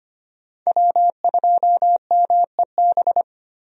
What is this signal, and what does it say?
Morse code, heard as a keyed tone, about 25 wpm, 700 Hz. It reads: W2MEB
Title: USA Call Signs - 5 Characters - Repeated 2x - 25wpm